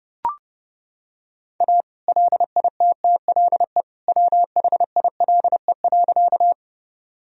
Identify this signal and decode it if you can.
Morse code, heard as a keyed tone, about 30 wpm, 700 Hz. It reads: A LITTLE WHILE.
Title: Sentences from Top 1000 Words - Review Part 1 - 30wpm